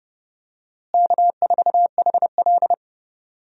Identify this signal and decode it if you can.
Morse code, heard as a keyed tone, about 32 wpm, 700 Hz. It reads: K4HL
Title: USA Call Signs - 4 Characters - Repeated 3x - 30wpm